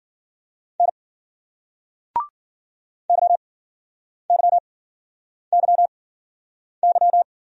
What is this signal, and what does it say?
Morse code, heard as a keyed tone, about 51 wpm, spaced as for 9 wpm, 700 Hz. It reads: N YYYY